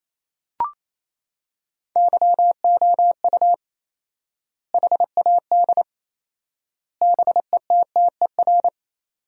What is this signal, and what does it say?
Morse code, heard as a keyed tone, about 28 wpm, 700 Hz. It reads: YOU HAD BETTER
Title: 4x Word-Spacing - Sentences from Top 400 Words - 28wpm